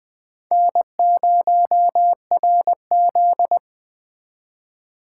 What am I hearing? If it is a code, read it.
Morse code, heard as a keyed tone, about 20 wpm, 700 Hz. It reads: N0RZ
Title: 2023 SOTA Chaser Call Signs - 20wpm